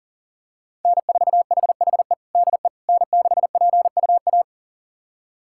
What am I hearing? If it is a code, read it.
Morse code, heard as a keyed tone, about 40 wpm, 700 Hz. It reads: N4HHE DE N6PUA
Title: USA Callsign DE Callsign - 5 Characters - 40wpm